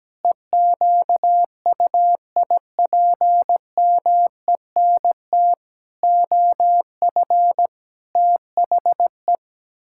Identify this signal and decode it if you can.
Morse code, heard as a keyed tone, about 17 wpm, 700 Hz. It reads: EQUIPMENT OF THE
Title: Pangram Challenge v2 - 17wpm